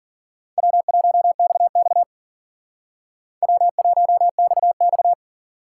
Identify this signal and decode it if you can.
Morse code, heard as a keyed tone, about 50 wpm, 690 Hz. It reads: W1XX W1XX
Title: Speed Racing - USA Call Signs 40wpm